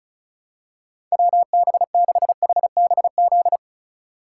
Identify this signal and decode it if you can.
Morse code, heard as a keyed tone, about 36 wpm, 700 Hz. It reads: WB6HBZ
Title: USA Call Signs - 6 Characters - 4x Character Spacing - 35wpm